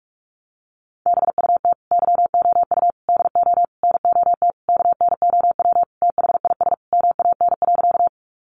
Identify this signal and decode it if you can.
Morse code, heard as a keyed tone, about 44 wpm, 700 Hz. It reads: BUT YOU DO NOT KNOW THIS MAN.
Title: Sentences from Top 600 Words 45wpm